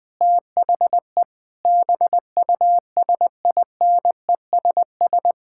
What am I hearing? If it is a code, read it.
Morse code, heard as a keyed tone, about 20 wpm, 700 Hz. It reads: THE BUSINESS
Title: Sentences from Top 1600 Words 20wpm